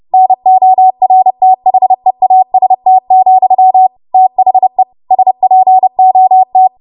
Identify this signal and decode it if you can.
Morse code, heard as a keyed tone, about 30 wpm, 750 Hz. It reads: NORTHEAST, THE SPOT
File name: Morse / 30WPM